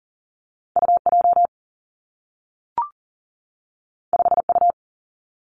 Morse code, heard as a keyed tone, about 41 wpm, 700 Hz. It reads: U J 5 U
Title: Two Challenging Letter-Number Combinations 40wpm